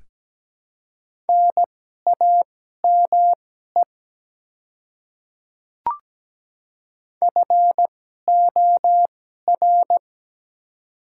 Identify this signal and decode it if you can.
Morse code, heard as a keyed tone, about 17 wpm, spaced as for 12 wpm, 700 Hz. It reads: NAME FOR
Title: 2x Character Spacing - Top 200 Words - 17wpm